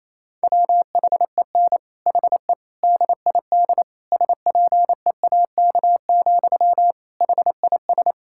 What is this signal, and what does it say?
Morse code, heard as a keyed tone, about 28 wpm, 700 Hz. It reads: WHEN HE DID SPEAK, HIS
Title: Sentences from Top 1600 Words 28wpm